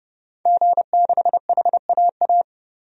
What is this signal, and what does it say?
Morse code, heard as a keyed tone, about 30 wpm, 700 Hz. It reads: G6HAA
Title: DXCC-100 Call Signs - 5 Characters - 30wpm